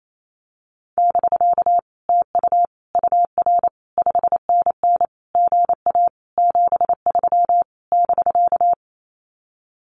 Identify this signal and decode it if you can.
Morse code, heard as a keyed tone, about 28 wpm, 700 Hz. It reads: <BK> TU UR 5NN GA 73 <BK>
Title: Common QSO Phrases Encore 28wpm